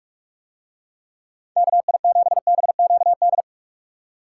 Morse code, heard as a keyed tone, about 47 wpm, 700 Hz. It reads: KI7BQD